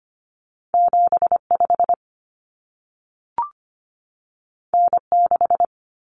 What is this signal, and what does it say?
Morse code, heard as a keyed tone, about 25 wpm, 700 Hz. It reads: 75 N6
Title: Common Cut Numbers - 2-Digit 25wpm